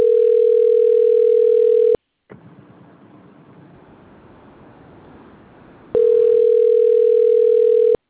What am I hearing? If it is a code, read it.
Ringback tone (North America), 440 + 480 Hz.